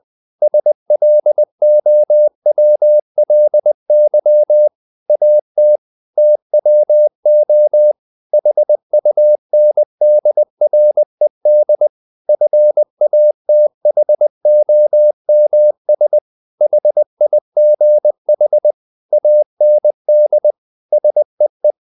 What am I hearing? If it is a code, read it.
Morse code, heard as a keyed tone, about 20 wpm, 600 Hz. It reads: SLOWLY AT TWO HUNDRED FATHOMS HIGH AND SEE
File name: the old man and the sea part 2 20wpm